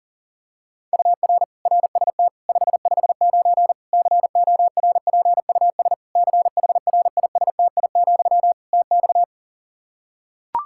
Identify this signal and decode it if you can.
Morse code, heard as a keyed tone, about 40 wpm, 700 Hz. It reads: UR RST 559 CORPUS CHRISTI, TX
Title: Practice QSO - Exchange 1 - Beginning - 40wpm